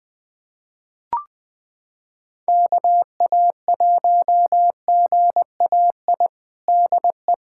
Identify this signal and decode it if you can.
Morse code, heard as a keyed tone, about 21 wpm, 700 Hz. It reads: KA1GAI DE
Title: USA Callsign DE Callsign - 6 Characters - 20wpm